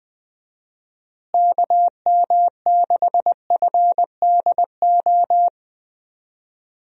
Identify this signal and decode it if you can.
Morse code, heard as a keyed tone, about 20 wpm, 700 Hz. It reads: KM6FDO
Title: USA Call Signs - 6 Characters - Repeated 2x - 20wpm